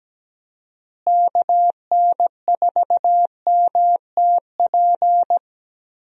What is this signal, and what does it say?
Morse code, heard as a keyed tone, about 17 wpm, 700 Hz. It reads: KN4MTP
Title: USA Call Signs - 6 Characters - Repeated 3x - 17wpm